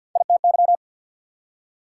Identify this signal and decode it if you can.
Morse code, heard as a keyed tone, about 69 wpm, spaced as for 59 wpm, 690 Hz. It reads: ITY
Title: Speed-Racing Prefixes and Suffixes 50wpm